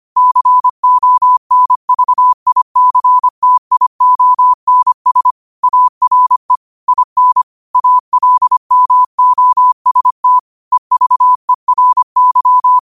Morse code, heard as a keyed tone, about 25 wpm, 1000 Hz. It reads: CONVICTIONS ARE IN ALMOST EVERY
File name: qotdfiles / 25wpm